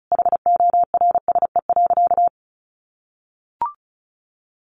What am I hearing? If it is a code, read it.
Morse code, heard as a keyed tone, about 34 wpm, 700 Hz. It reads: HORSE.
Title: Sentences from Top 1100 Words 35wpm